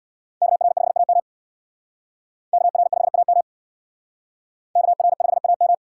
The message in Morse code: KR5AM KR5AM KR5AM

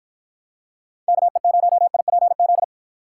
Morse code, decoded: KE0IPZ

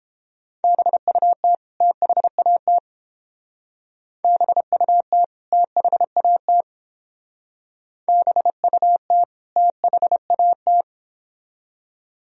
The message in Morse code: BUT THAT BUT THAT BUT THAT

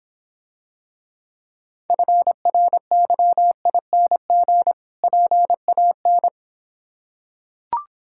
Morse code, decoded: FRYING PAN

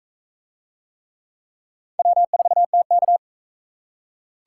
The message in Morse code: W4TK

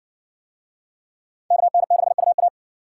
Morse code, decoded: KN6UA

